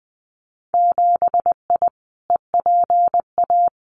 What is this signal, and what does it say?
Morse code, heard as a keyed tone, about 20 wpm, 700 Hz. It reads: 7I EPA
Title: Winter Field Day Exchanges Encore 20wpm